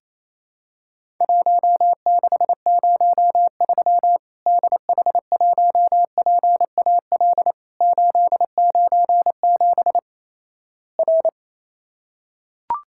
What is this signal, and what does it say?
Morse code, heard as a keyed tone, about 28 wpm, 700 Hz. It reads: 1603 DH1PAL 897 R
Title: Worked All Europe DX Contest - QTC Exchange - 28wpm